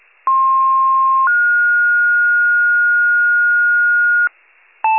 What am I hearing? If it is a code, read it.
Two-tone page, heard as a one-second tone, then a three-second tone, about 1060 then 1430 Hz, with noise. It is heard from 0.3 to 4.3 s.